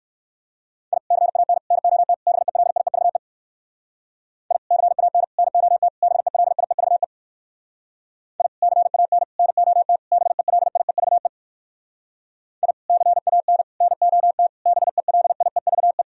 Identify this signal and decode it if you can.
Morse code, heard as a keyed tone, about 58 wpm, 700 Hz. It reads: I CAN NOT BELIEVE I CAN NOT BELIEVE I CAN NOT BELIEVE I CAN NOT BELIEVE